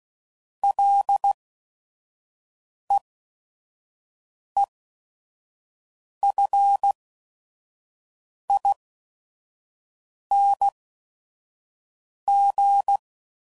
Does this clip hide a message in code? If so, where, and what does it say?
Yes: Morse code from 0.6 to 13.0 s, heard as a keyed tone, about 16 wpm, spaced as for 5 wpm, 800 Hz. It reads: LEEFING